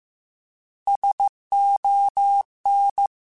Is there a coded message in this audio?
Yes: Morse code from 0.9 to 3.1 s, heard as a keyed tone, about 15 wpm, 800 Hz. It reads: SON